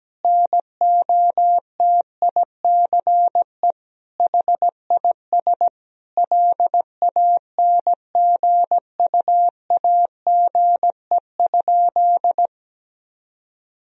Morse code, heard as a keyed tone, about 17 wpm, 700 Hz. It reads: NOTICE HIS LANGUAGE?
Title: Sentences from Top 1000 Words - Review Part 3 - 17wpm